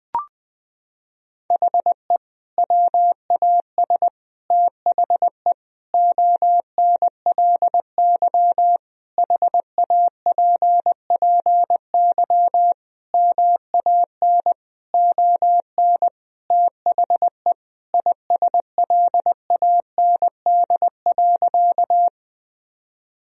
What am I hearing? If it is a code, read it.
Morse code, heard as a keyed tone, about 20 wpm, 700 Hz. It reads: HE WAS THE ONLY HAPPY MAN ON THE ISLAND.
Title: Sentences from Top 1000 Words - Review Part 3 - 20wpm